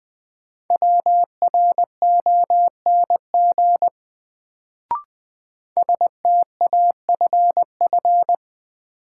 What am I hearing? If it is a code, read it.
Morse code, heard as a keyed tone, about 20 wpm, 700 Hz. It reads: WRONG STAFF
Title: Sets of 5 Words Encore - New Word Drill - 20wpm